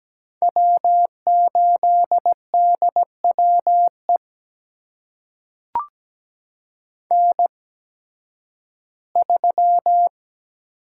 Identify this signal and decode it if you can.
Morse code, heard as a keyed tone, about 17 wpm, spaced as for 11 wpm, 700 Hz. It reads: W8DWE N3
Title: USA Call Signs - 5 Characters - 8x Character Spacing - 17wpm